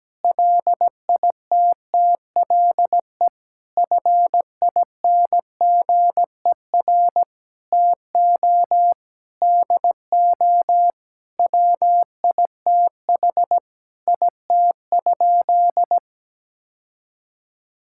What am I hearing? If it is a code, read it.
Morse code, heard as a keyed tone, about 17 wpm, 700 Hz. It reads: LITTLE FINGER TO DO WITH IT?
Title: Sentences from Top 1600 Words 17wpm